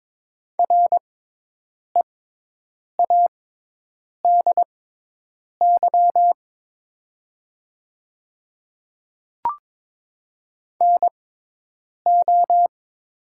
Morse code, heard as a keyed tone, about 23 wpm, spaced as for 8 wpm, 700 Hz. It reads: READY NO
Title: 6x Character Spacing - Top 500 Words - 22wpm